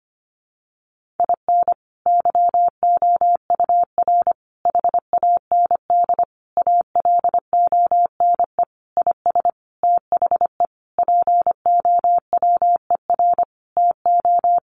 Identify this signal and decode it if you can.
Morse code, heard as a keyed tone, about 25 wpm, 700 Hz. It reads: IN YOUR HAND ALONE IS THE POWER TO
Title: Sentences from Top 1400 Words 25wpm